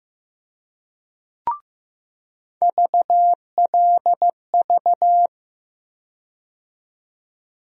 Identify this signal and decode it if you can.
Morse code, heard as a keyed tone, about 15 wpm, 700 Hz. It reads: VLV
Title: USA Call Sign Postfixes - 3 Letters 15wpm